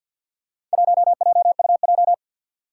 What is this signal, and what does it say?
Morse code, heard as a keyed tone, about 47 wpm, 700 Hz. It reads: 1JUJ
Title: Four Challenging Letter-Number Combinations 50wpm